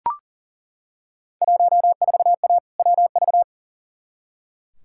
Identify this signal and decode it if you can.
Morse code, heard as a keyed tone, about 40 wpm, 700 Hz. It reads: 14A WV
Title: Field Day Exchanges Encore - 40wpm